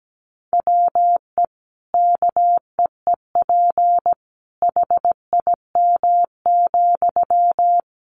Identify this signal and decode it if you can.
Morse code, heard as a keyed tone, about 17 wpm, 700 Hz. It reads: WE KEEP HIM,
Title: Sentences from Top 600 Words 17wpm